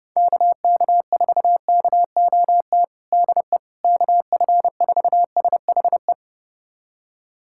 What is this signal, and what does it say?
Morse code, heard as a keyed tone, about 30 wpm, 700 Hz. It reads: KK4KOT DE KF4SHE
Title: USA Callsign DE Callsign 30wpm